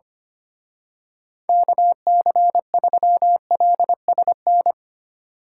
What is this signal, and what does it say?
Morse code, heard as a keyed tone, about 29 wpm, 700 Hz. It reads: KC3LSN
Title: Speed Racing - USA 2x3 Call Signs 25wpm